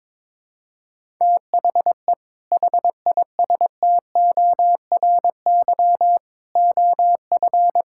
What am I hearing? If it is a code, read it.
Morse code, heard as a keyed tone, about 22 wpm, 700 Hz. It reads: THE HISTORY OF
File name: practice / 22wpm / Sentences with 3-Word Phrases - 22wpm